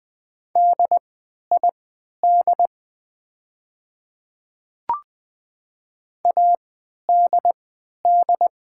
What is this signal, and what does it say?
Morse code, heard as a keyed tone, about 20 wpm, spaced as for 11 wpm, 700 Hz. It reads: DID ADD
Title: 3x Character Spacing - Top 200 Words - 20wpm